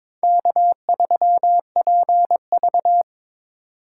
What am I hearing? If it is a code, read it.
Morse code, heard as a keyed tone, about 25 wpm, 700 Hz. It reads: K3PV
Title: Speed Racing - USA 1x2 Call Signs 22wpm